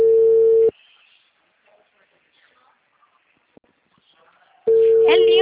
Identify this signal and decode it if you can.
Ringback tone (China, 450 Hz).